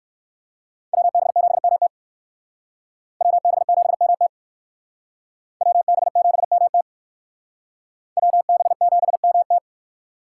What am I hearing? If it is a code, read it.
Morse code, heard as a keyed tone, about 56 wpm, 700 Hz. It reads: WB7MT WB7MT WB7MT WB7MT